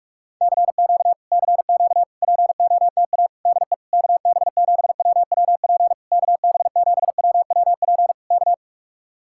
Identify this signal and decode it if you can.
Morse code, heard as a keyed tone, about 44 wpm, 700 Hz. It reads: CQ CQ POTA DE KB7WWP KB7WWP K